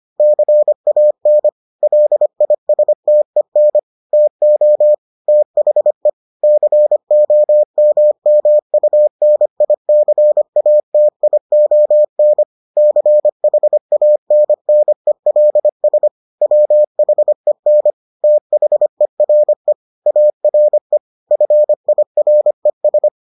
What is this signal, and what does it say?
Morse code, heard as a keyed tone, about 25 wpm, 600 Hz. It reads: CAN LISTEN TO THE COMMUNICATION CHANNELS WHEN THERE ARE FIRES